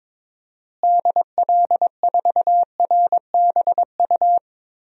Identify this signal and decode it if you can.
Morse code, heard as a keyed tone, about 22 wpm, 700 Hz. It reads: DL4RBU
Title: DXCC-100 Call Signs - 6 Characters - Repeated 2x - 22wpm